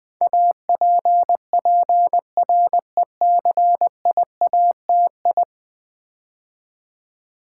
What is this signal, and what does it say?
Morse code, heard as a keyed tone, about 20 wpm, 700 Hz. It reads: APPRECIATI
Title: Build A Word - 5 Letters At A Time - 12-Letter Words - 20wpm